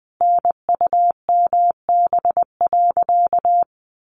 Morse code, heard as a keyed tone, about 20 wpm, 700 Hz. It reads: NUMB.
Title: A Sound Of Thunder 20wpm